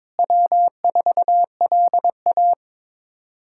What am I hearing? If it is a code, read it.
Morse code, heard as a keyed tone, about 22 wpm, 700 Hz. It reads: W4LA
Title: USA Callsign DE Callsign - 4 Characters - 22wpm